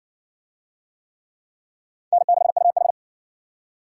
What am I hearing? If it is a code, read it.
Morse code, heard as a keyed tone, about 60 wpm, 700 Hz. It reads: N6UH